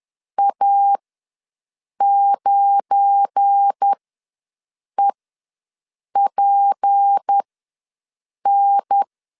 Morse code, heard as a keyed tone, about 10 wpm, spaced as for 6 wpm, 800 Hz. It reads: A9EPN
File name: LEZIONI / 8 YJ45 / 8 YJ4 ws 10wpm 4020